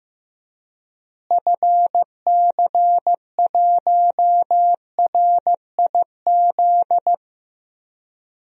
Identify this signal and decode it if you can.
Morse code, heard as a keyed tone, about 15 wpm, 700 Hz. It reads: FC1RIZ
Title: DXCC-100 Call Signs - 6 Characters - Repeated 2x - 15wpm